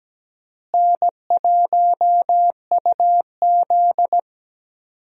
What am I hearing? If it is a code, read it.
Morse code, heard as a keyed tone, about 17 wpm, 700 Hz. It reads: N1UZ
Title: USA Call Signs - 4 Characters - Repeated 2x - 17wpm